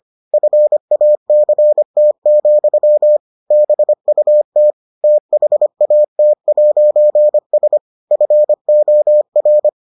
Morse code, heard as a keyed tone, about 25 wpm, 600 Hz. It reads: FACT, BUT THAT'S FOR